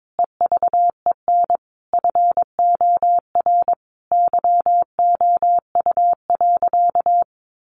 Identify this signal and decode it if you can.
Morse code, heard as a keyed tone, about 22 wpm, 700 Hz. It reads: EVEN FOR YOU.
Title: Sentences from Top 700 Words 22wpm